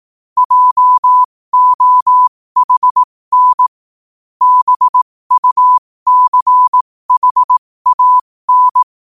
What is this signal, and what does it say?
Morse code, heard as a keyed tone, about 18 wpm, spaced as for 15 wpm, 1000 Hz. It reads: JOHN BUCHAN
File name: qotdfiles / 15wpm